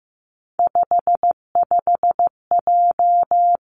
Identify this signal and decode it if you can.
Morse code, heard as a keyed tone, about 15 wpm, 700 Hz. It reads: 55J
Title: Three Challenging Letter-Number Combinations 15wpm